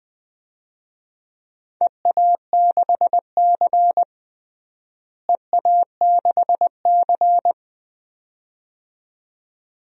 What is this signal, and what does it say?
Morse code, heard as a keyed tone, about 21 wpm, 700 Hz. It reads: EA6C EA6C
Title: DXCC-100 Call Signs - 4 Characters - Repeated 2x - 20wpm